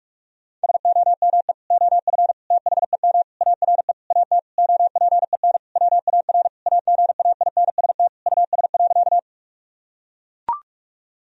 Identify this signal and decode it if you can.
Morse code, heard as a keyed tone, about 41 wpm, 700 Hz. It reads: SOME OF THEM ARE AT OPEN WAR AGAINST US.
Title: Sentences from Top 1000 Words - Review Part 2 - 45wpm